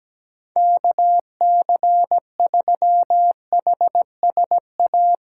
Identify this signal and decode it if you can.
Morse code, heard as a keyed tone, about 17 wpm, 700 Hz. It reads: KC3HSA
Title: USA Call Signs - 6 Characters - 3x Character Spacing - 17wpm